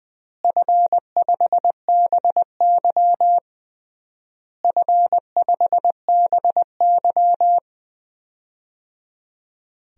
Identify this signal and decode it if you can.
Morse code, heard as a keyed tone, about 21 wpm, 700 Hz. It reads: F5BY F5BY
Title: DXCC-100 Call Signs - 4 Characters - Repeated 3x - 20wpm